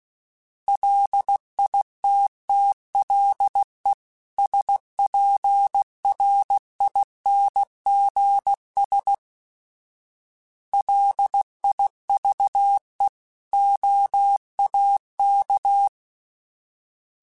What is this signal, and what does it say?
Morse code, heard as a keyed tone, about 16 wpm, 800 Hz. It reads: LITTLE SPRINGS LIVE OAK
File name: fromSMs / FromSMs mp3s / Cities@15